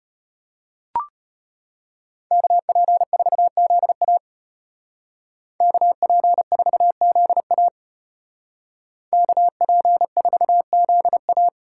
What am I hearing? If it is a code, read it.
Morse code, heard as a keyed tone, about 33 wpm, 700 Hz. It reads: KP4ZA KP4ZA KP4ZA